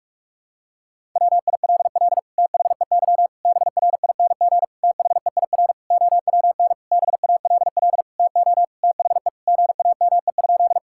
Morse code, heard as a keyed tone, about 46 wpm, 700 Hz. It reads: WILL THEY BRING THEIR OWN BALL TO THE GAME?
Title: Sentences from Top 1000 Words - Review Part 6 - 45wpm